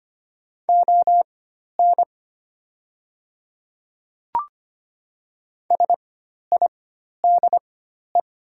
Morse code, heard as a keyed tone, about 25 wpm, spaced as for 12 wpm, 700 Hz. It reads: ON SIDE
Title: 4x Character Spacing - Top 100 Words - 25wpm